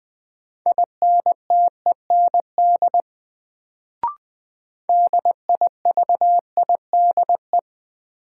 Morse code, heard as a keyed tone, about 20 wpm, 700 Hz. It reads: INTEND DIVIDE
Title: Sentences from Top 1600 Words - New Word Drill - 20wpm